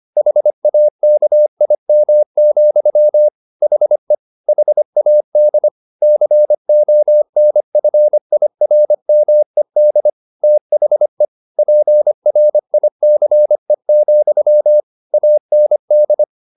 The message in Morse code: HAKIM, HE HAD CONFIRMED THE PRICE, AND

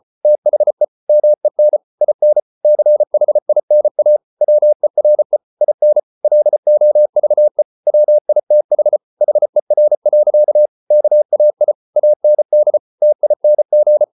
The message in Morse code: THE MEN IN CHINA WERE IN LOVE WITH HER. KAI AND TING

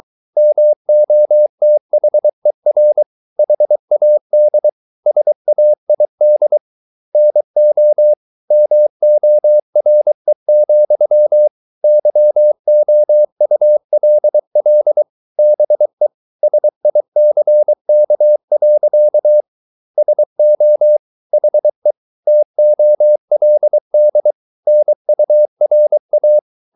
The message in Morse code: MOTHER HAD SAID NO MORE, YOULL BE SICK. SO HE TOLD NURA

